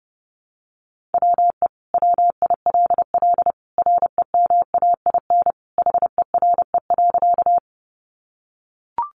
Morse code, heard as a keyed tone, about 30 wpm, 700 Hz. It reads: WE WILL REMAIN HERE.
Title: Sentences from Top 1200 Words 30wpm